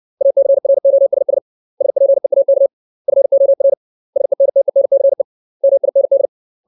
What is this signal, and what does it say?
Morse code, heard as a keyed tone, about 60 wpm, 550 Hz. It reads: ACROSS SPEAK FOR SENTENCE MIND
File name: 632 words / 632 Words 60WPM